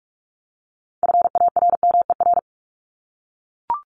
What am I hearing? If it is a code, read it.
Morse code, heard as a keyed tone, about 44 wpm, 700 Hz. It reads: FARMER
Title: Top 1751-2000 Words 45wpm